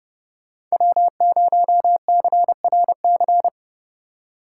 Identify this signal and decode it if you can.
Morse code, heard as a keyed tone, about 39 wpm, 700 Hz. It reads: W0CRC